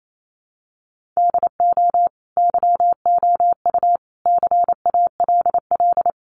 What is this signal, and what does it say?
Morse code, heard as a keyed tone, about 28 wpm, 700 Hz. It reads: DO YOU CALL
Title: Sentences from Top 1500 Words 28wpm